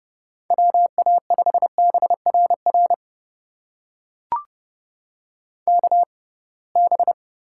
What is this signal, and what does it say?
Morse code, heard as a keyed tone, about 32 wpm, spaced as for 23 wpm, 700 Hz. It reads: WA5BRR KB
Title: USA Call Signs - 6 Characters - 6x Character Spacing - 30wpm